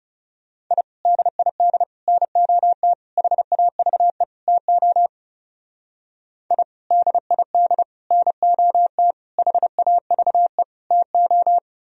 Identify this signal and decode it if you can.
Morse code, heard as a keyed tone, about 37 wpm, 700 Hz. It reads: I DID NOT HAVE TO I DID NOT HAVE TO
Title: Speed-Racing - Sets of 5 Words 30wpm